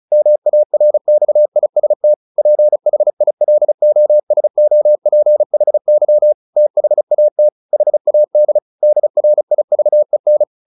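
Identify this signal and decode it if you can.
Morse code, heard as a keyed tone, about 35 wpm, 600 Hz. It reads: MARXIST PHILOSOPHY THAT HAD DRIVEN
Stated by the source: Morse practice at 35 wpm from CW med Gnister, sending Ken Follett's novel Never in English.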